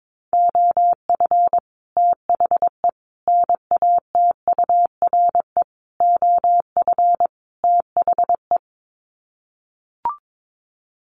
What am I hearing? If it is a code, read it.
Morse code, heard as a keyed tone, about 22 wpm, 700 Hz. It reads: OF THE NATURE OF THE
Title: Sets of 5 Words Encore2 22wpm